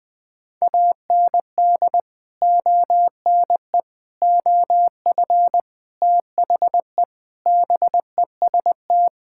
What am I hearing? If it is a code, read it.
Morse code, heard as a keyed tone, about 19 wpm, 700 Hz. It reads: AND ONE OF THE BEST
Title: Sentences with 4-Word Phrases - Spoken Beginning - 20wpm